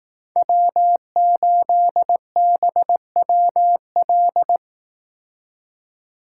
Morse code, heard as a keyed tone, about 18 wpm, 700 Hz. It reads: W8BWL
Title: Speed Racing - USA 1x3 Call Signs 15wpm